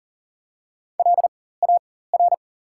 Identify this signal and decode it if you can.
Morse code, heard as a keyed tone, about 40 wpm, spaced as for 19 wpm, 700 Hz. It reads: LAR
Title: 4x Character Spacing - Top 200 Words - 40wpm